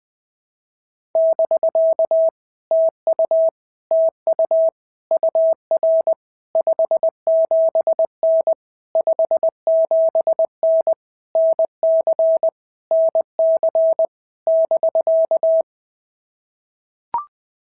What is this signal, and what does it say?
Morse code, heard as a keyed tone, about 19 wpm, 650 Hz. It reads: <BK> TU TU UR 57N 57N NC NC <BK>
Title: POTA QSO Practice - 20wpm